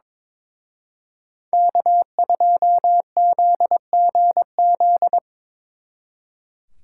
Morse code, heard as a keyed tone, about 25 wpm, 700 Hz. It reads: K2ZGZ
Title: Speed Racing - USA Call Signs 22wpm